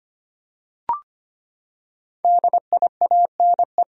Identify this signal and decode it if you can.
Morse code, heard as a keyed tone, about 25 wpm, 700 Hz. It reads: DIANE
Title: USA Common Names - Max 5 Letters - 25wpm